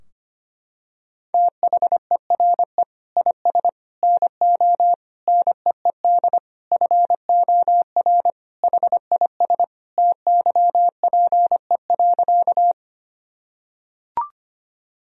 Morse code, heard as a keyed tone, about 25 wpm, 700 Hz. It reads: THERE IS NO NEED FOR HIS TYPE.
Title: Sentences from Top 1000 Words - Review Part 3 - 25wpm